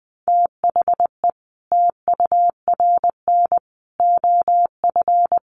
Morse code, heard as a keyed tone, about 20 wpm, 700 Hz. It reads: THE TURN OF
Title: Sets of 4 Words 20wpm